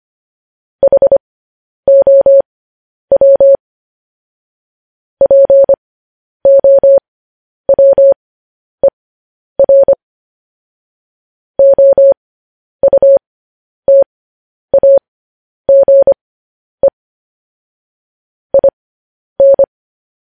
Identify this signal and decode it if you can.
Morse code, heard as a keyed tone, about 25 wpm, spaced as for 10 wpm, 550 Hz. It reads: HOW POWER OUTAGE IN
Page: news headlines in morse code at 10 wpm